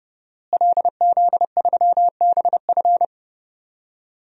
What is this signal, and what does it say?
Morse code, heard as a keyed tone, about 30 wpm, 700 Hz. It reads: LZ3BF
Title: DXCC-100 Call Signs - 5 Characters - Repeated 3x - 30wpm